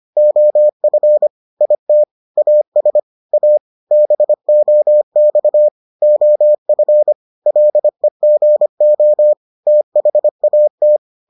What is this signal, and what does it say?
Morse code, heard as a keyed tone, about 25 wpm, 600 Hz. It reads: OF IT AS A BOX OF LEGO THAT